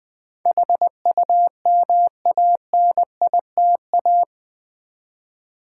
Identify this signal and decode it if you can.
Morse code, heard as a keyed tone, about 20 wpm, 700 Hz. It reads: HUMANITA
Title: Build A Word - 4 Letters At A Time - 12-Letter Words - 20wpm